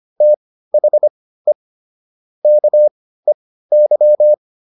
Morse code, heard as a keyed tone, about 25 wpm, spaced as for 15 wpm, 600 Hz. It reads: THE KEY